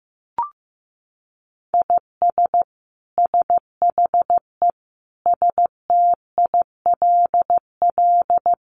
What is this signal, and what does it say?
Morse code, heard as a keyed tone, about 15 wpm, 700 Hz. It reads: IS SHE STILL
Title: Sentences from Top 1100 Words 15wpm